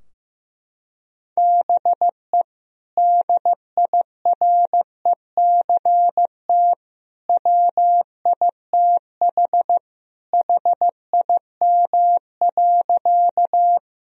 Morse code, heard as a keyed tone, about 15 wpm, 700 Hz. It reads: BE DIRECT WITH HIM.
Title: Sentences from Top 1000 Words - Review Part 2 - 15wpm